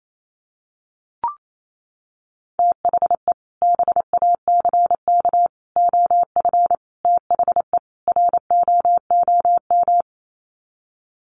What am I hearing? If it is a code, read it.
Morse code, heard as a keyed tone, about 28 wpm, 700 Hz. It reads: THE BACK OF THE ROOM
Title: Sets of 5 Words Encore 28wpm